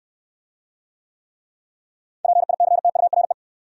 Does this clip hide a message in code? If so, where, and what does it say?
Yes: Morse code from 2.2 to 3.3 s, heard as a keyed tone, about 67 wpm, 700 Hz. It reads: PICTURE